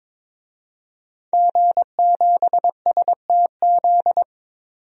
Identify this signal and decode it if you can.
Morse code, heard as a keyed tone, about 22 wpm, 700 Hz. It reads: G7STZ
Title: DXCC-100 Call Signs - 5 Characters - 22wpm